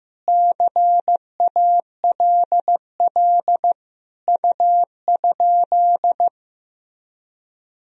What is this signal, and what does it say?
Morse code, heard as a keyed tone, about 15 wpm, 700 Hz. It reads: CALL U?
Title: Sentences from Top 100 Words and Abbreviations 15wpm